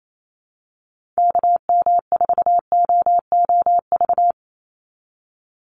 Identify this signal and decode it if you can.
Morse code, heard as a keyed tone, about 28 wpm, 700 Hz. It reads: KM4OOV
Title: USA Call Signs 28wpm